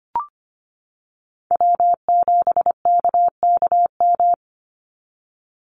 Morse code, heard as a keyed tone, about 25 wpm, 700 Hz. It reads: W7KKM
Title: NA SOTA Chaser Callsigns 25wpm